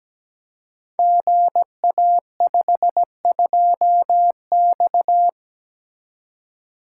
Morse code, heard as a keyed tone, about 17 wpm, 700 Hz. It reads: GA52X
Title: Code Groups - 5 Letter:Numbers - 17wpm